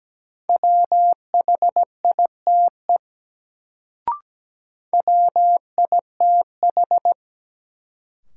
Morse code, heard as a keyed tone, about 17 wpm, 700 Hz. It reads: WHITE WITH
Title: Sets of 2 Words - New Word Drill - 17wpm